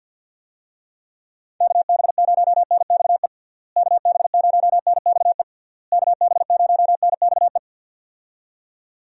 Morse code, heard as a keyed tone, about 50 wpm, 700 Hz. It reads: KB0NXE KB0NXE KB0NXE
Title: USA Call Signs - 6 Characters - Repeated 3x - 50wpm